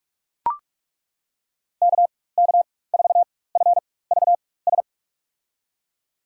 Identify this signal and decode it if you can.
Morse code, heard as a keyed tone, about 42 wpm, spaced as for 20 wpm, 700 Hz. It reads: KK4FVS
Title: USA Call Signs - 6 Characters - 4x Character Spacing - 45wpm